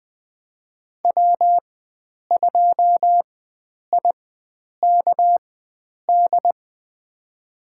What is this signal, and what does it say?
Morse code, heard as a keyed tone, about 20 wpm, spaced as for 9 wpm, 700 Hz. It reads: W2IKD
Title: USA Call Signs - 4x Character Spacing - 20wpm